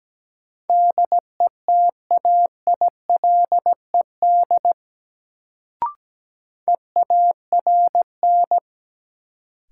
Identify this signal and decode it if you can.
Morse code, heard as a keyed tone, about 17 wpm, 700 Hz. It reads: DETAILED EARN
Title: Sentences from Top 1500 Words - New Word Drill - 17wpm